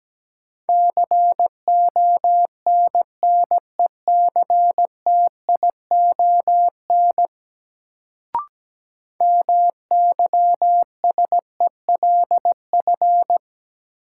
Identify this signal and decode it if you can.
Morse code, heard as a keyed tone, about 17 wpm, 700 Hz. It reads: CONNECTION MYSELF
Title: Sentences from Top 1300 Words - New Word Drill Alt - 17wpm